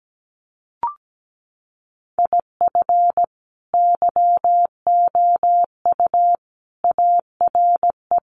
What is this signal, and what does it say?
Morse code, heard as a keyed tone, about 17 wpm, 700 Hz. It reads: IF YOU ARE
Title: Sets of 5 Words Encore2 17wpm